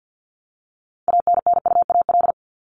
Morse code, heard as a keyed tone, about 50 wpm, 700 Hz. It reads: ANNUAL
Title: Top 1251-1500 Words 50wpm